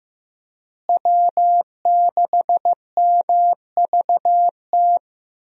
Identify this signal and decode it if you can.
Morse code, heard as a keyed tone, about 15 wpm, 700 Hz. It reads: W6MVT
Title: USA Call Signs - 5 Characters - Repeated 2x - 15wpm